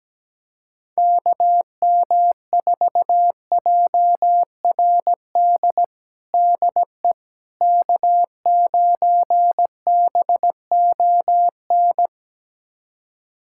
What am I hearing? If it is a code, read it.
Morse code, heard as a keyed tone, about 17 wpm, 700 Hz. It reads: KM4JRD DE K9BON
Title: USA Callsign DE Callsign 17wpm